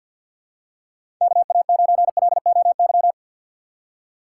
Morse code, heard as a keyed tone, about 54 wpm, 700 Hz. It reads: KA9LOY
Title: USA Call Signs - 6 Characters - 6x Character Spacing - 50wpm